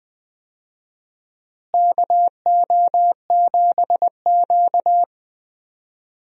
Morse code, heard as a keyed tone, about 20 wpm, 700 Hz. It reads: KO7Q
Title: USA Call Signs - 4 Characters - Repeated 2x - 20wpm